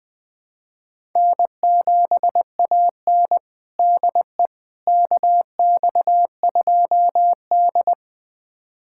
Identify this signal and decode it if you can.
Morse code, heard as a keyed tone, about 20 wpm, 700 Hz. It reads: N7AN DE KX2D
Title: USA Callsign DE Callsign - 4 Characters - 20wpm